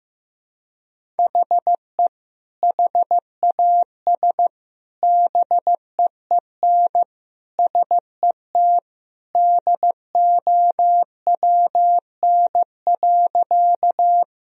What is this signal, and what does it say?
Morse code, heard as a keyed tone, about 15 wpm, 700 Hz. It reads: HE HAS BEEN SET DOWN.
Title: Sentences from Top 1000 Words - Review Part 1 - 15wpm